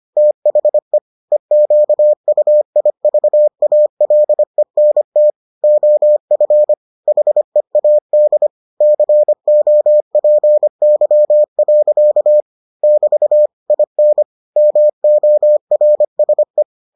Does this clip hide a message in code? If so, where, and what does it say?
Yes: Morse code from 0.2 to 16.6 s, heard as a keyed tone, about 25 wpm, 600 Hz. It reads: THE EQUIVALENT OF HEAD COPY. <BT> IN MORSE